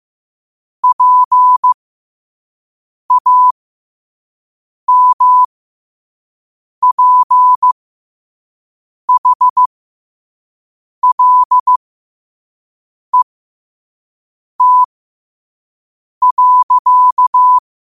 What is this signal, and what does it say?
Morse code, heard as a keyed tone, about 15 wpm, spaced as for 5 wpm, 1000 Hz. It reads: PAMPHLET.
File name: qotdfiles / 5wpm